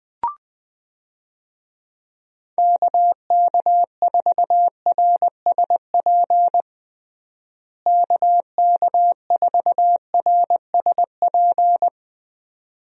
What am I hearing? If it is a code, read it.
Morse code, heard as a keyed tone, about 20 wpm, 700 Hz. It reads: KK4RSP KK4RSP
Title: USA Call Signs - 6 Characters - Repeated 2x - 20wpm